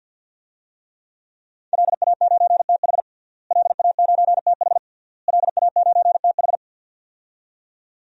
Morse code, heard as a keyed tone, about 47 wpm, 700 Hz. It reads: LA9TH LA9TH LA9TH
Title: DXCC-100 Call Signs - 5 Characters - Repeated 3x - 50wpm